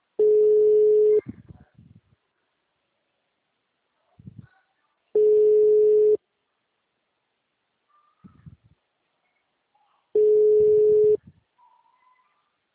Ringback tone (Europe, 425 Hz).